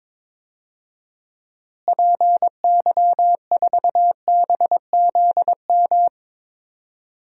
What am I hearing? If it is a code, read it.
Morse code, heard as a keyed tone, about 22 wpm, 700 Hz. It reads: PY4BZM
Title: DXCC-100 Call Signs - 6 Characters - Repeated 3x - 22wpm